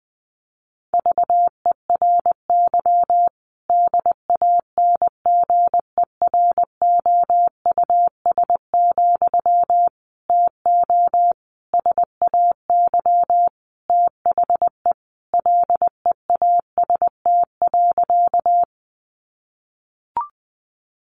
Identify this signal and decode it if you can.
Morse code, heard as a keyed tone, about 19 wpm, 700 Hz. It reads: VERY DANGEROUS, TO SAY THE LEAST.
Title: Sentences from Top 1200 Words 20wpm